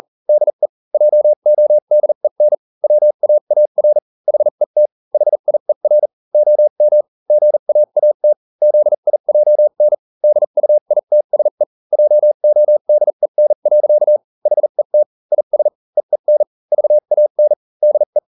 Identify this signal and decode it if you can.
Morse code, heard as a keyed tone, about 40 wpm, spaced as for 35 wpm, 600 Hz. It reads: DE JODEN WAAR HET HIER OM GAAT ZIJN DUITSE JODEN. HET IS EEN VAN DE